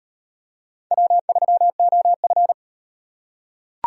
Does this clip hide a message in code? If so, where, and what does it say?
Yes: Morse code from 0.9 to 2.5 s, heard as a keyed tone, about 36 wpm, 700 Hz. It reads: W3OF